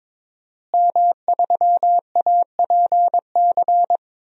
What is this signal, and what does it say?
Morse code, heard as a keyed tone, about 22 wpm, 700 Hz. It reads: M3APC
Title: DXCC-100 Call Signs - 5 Characters - Repeated 3x - 22wpm